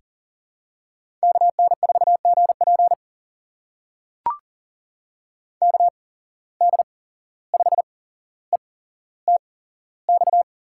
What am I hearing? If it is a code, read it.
Morse code, heard as a keyed tone, about 40 wpm, spaced as for 16 wpm, 700 Hz. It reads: KN4GP KD5ETX
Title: USA Call Signs - 8x Character Spacing - 40wpm